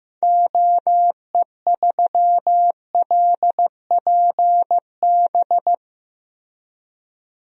Morse code, heard as a keyed tone, about 15 wpm, 700 Hz. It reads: OE3LPB
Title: DXCC-100 Call Signs - 6 Characters - Repeated 2x - 15wpm